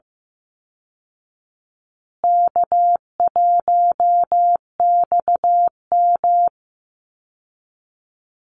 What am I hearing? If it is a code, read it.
Morse code, heard as a keyed tone, about 15 wpm, 700 Hz. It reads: K1XM
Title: Easy Call Signs Repeated 2x - 15wpm